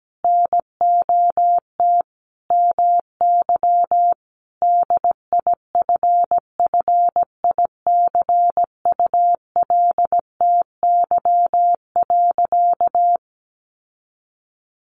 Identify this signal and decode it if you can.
Morse code, heard as a keyed tone, about 17 wpm, 700 Hz. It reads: NOT MY DIFFICULTY.
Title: Sentences from Top 1500 Words 17wpm